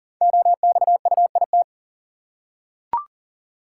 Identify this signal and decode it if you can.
Morse code, heard as a keyed tone, about 38 wpm, 690 Hz. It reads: OXUIT
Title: Code Groups - 5 Letters - 40wpm